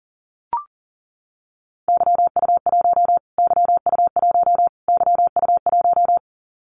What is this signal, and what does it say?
Morse code, heard as a keyed tone, about 40 wpm, 700 Hz. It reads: YU1 YU1 YU1
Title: Callsign Prefixes for Top 25 DXCC - Repeated 3x - 40wpm